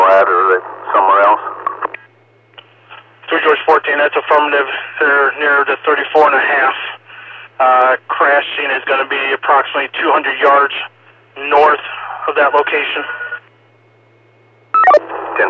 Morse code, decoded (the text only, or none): none